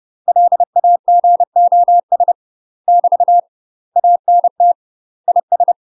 Morse code, heard as a keyed tone, about 30 wpm, 700 Hz. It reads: LAGOS <BT> ANT IS
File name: UK MASTER QSOs TRAINING / 30WPM / UK MASTER QSOs 0022